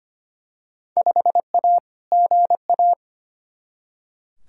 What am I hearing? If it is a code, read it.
Morse code, heard as a keyed tone, about 25 wpm, 700 Hz. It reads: 5A GA
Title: Field Day Exchanges Encore - 25wpm